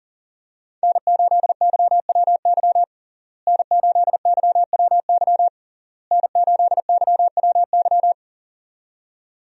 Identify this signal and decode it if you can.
Morse code, heard as a keyed tone, about 40 wpm, 700 Hz. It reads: N8YWY N8YWY N8YWY
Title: USA Call Signs - 5 Characters - Repeated 3x - 40wpm